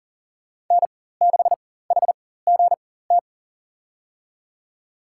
Morse code, heard as a keyed tone, about 40 wpm, spaced as for 19 wpm, 700 Hz. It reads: N6HGT
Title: USA Call Signs - 5 Characters - 4x Character Spacing - 40wpm